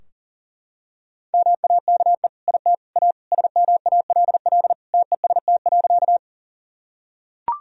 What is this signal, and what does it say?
Morse code, heard as a keyed tone, about 40 wpm, 700 Hz. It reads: MAKE IT A SMALL TEST.
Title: Sentences from Top 1000 Words - Review Part 3 - 40wpm